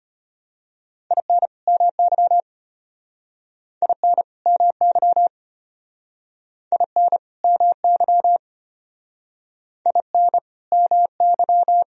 Morse code, heard as a keyed tone, about 33 wpm, 700 Hz. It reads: IN MY IN MY IN MY IN MY